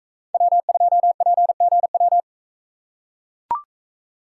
Morse code, heard as a keyed tone, about 42 wpm, 700 Hz. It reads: W2PGW